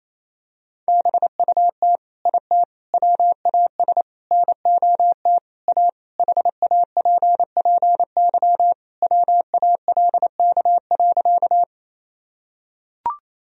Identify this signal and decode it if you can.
Morse code, heard as a keyed tone, about 28 wpm, 700 Hz. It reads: BUT IT WAS NOT A HAPPY WALK.
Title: Sentences from Top 1000 Words - Review Part 1 - 28wpm